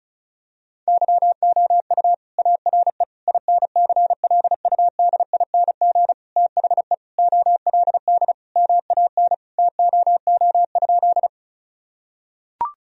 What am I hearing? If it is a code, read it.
Morse code, heard as a keyed tone, about 34 wpm, 700 Hz. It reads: YOU ARE INCLUDING THE OLD MAN TOO?
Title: Sentences from Top 1000 Words - Review Part 4 - 35wpm